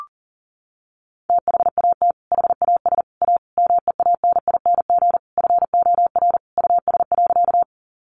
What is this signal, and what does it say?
Morse code, heard as a keyed tone, about 40 wpm, 700 Hz. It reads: THAT HAS A MEANING FOR US.
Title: Sentences from Top 900 Words 40wpm